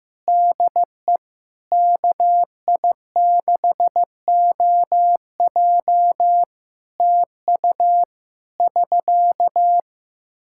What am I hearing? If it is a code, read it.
Morse code, heard as a keyed tone, about 15 wpm, 700 Hz. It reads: DE KI6OJ TU <SK>
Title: K1USN SST QSO Practice 15wpm